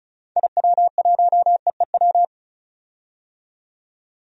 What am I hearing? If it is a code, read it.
Morse code, heard as a keyed tone, about 36 wpm, 700 Hz. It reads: IW1EEW
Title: DXCC-100 Call Signs - 6 Characters - 35wpm